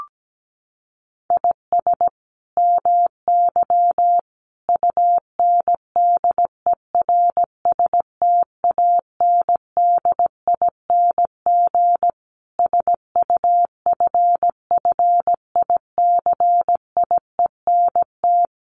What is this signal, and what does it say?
Morse code, heard as a keyed tone, about 17 wpm, 700 Hz. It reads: IS MY UNDERSTANDING SUFFICIENT
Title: Sentences from Top 1400 Words 17wpm